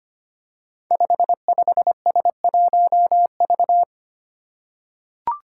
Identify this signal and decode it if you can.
Morse code, heard as a keyed tone, about 25 wpm, 700 Hz. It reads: 55S1V
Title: Five Challenging Letter-Number Combinations 25wpm